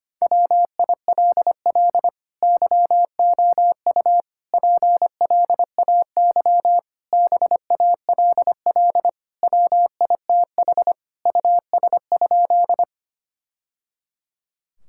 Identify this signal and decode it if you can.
Morse code, heard as a keyed tone, about 25 wpm, 700 Hz. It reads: WILL YOU PLAY BALL WITH US?
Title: Sentences from Top 1000 Words - Review Part 6 - 25wpm